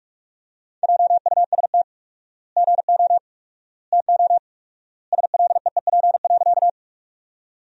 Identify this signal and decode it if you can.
Morse code, heard as a keyed tone, about 45 wpm, 700 Hz. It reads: JUST GO TO SLEEP.
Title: 4x Word-Spacing - Sentences from Top 500 Words - 45wpm